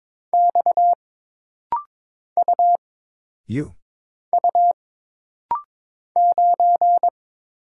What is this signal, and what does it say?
Morse code, heard as a keyed tone, about 22 wpm, 700 Hz. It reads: X U U 9
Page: This rapid-fire timing is geared to push your ICR (Instant Character Recognition) skills.